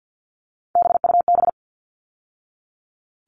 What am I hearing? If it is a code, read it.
Morse code, heard as a keyed tone, about 50 wpm, 700 Hz. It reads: BUB
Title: All Cut Numbers - 3-Digit 50wpm